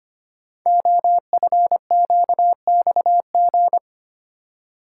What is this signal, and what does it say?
Morse code, heard as a keyed tone, about 25 wpm, 700 Hz. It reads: OFQXG
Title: Code Groups - 5 Letters - 25wpm